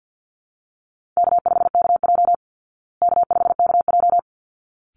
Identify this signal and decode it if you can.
Morse code, heard as a keyed tone, about 48 wpm, 700 Hz. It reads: K5KJ K5KJ
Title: Easy Call Signs Repeated 2x - 50wpm